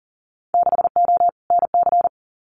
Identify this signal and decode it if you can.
Morse code, heard as a keyed tone, about 38 wpm, 700 Hz. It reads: 6O NC
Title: Winter Field Day Exchanges 40wpm